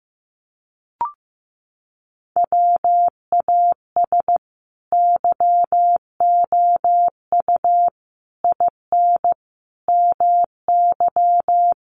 Morse code, heard as a keyed tone, about 15 wpm, 700 Hz. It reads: WAS YOU IN MY
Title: Sentences from Top 300 Words 15wpm